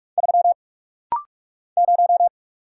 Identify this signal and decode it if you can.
Morse code, heard as a keyed tone, about 44 wpm, 700 Hz. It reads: 3 0
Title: Single Letter-Number - Rapid-Fire 45wpm